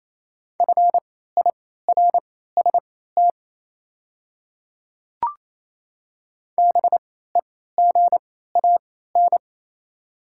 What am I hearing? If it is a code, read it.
Morse code, heard as a keyed tone, about 27 wpm, spaced as for 16 wpm, 700 Hz. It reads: FIRST BEGAN
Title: 3x Character Spacing - Top 300 Words - 28wpm